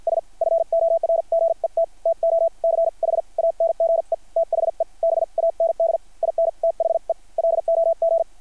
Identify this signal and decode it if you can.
Morse code, heard as a keyed tone, about 50 wpm, spaced as for 43 wpm, 660 Hz. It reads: S FORGET TO CHANGE THE BAND IN THE LOG
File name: archived / CRSnet